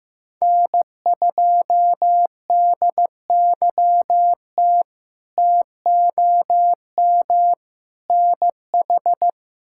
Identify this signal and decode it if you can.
Morse code, heard as a keyed tone, about 15 wpm, 700 Hz. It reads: N2DYT TOM NH
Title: K1USN SST QSO Practice - 15wpm